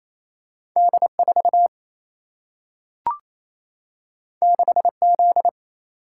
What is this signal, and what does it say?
Morse code, heard as a keyed tone, about 28 wpm, 700 Hz. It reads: D4 6Z
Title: Single-Letter Single-Number 28wpm